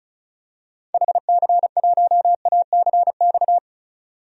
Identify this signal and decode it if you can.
Morse code, heard as a keyed tone, about 34 wpm, 700 Hz. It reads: HC1ACX
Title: DXCC-100 Call Signs - 6 Characters - 35wpm